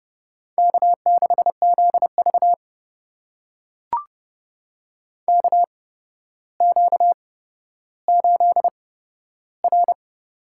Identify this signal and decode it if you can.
Morse code, heard as a keyed tone, about 30 wpm, spaced as for 13 wpm, 700 Hz. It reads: K6ZV KQ8R
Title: USA Call Signs - 4 Characters - 8x Character Spacing - 30wpm